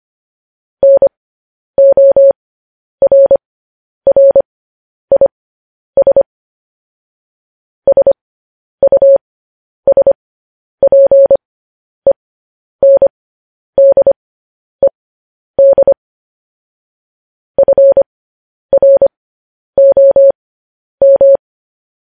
Morse code, heard as a keyed tone, about 25 wpm, spaced as for 10 wpm, 550 Hz. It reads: NORRIS SUSPENDED FROM
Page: news headlines in morse code at 10 wpm